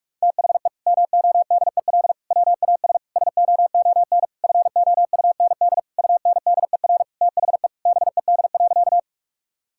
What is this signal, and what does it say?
Morse code, heard as a keyed tone, about 46 wpm, 700 Hz. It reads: THE MODEL WAS SOON FOUND UNDER THE BED.
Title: Sentences from Top 1000 Words - Review Part 5 - 45wpm